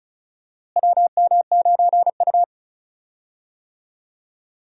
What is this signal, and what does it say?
Morse code, heard as a keyed tone, about 34 wpm, 700 Hz. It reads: WM9U
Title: USA Call Signs - 4 Characters - Repeated 2x - 35wpm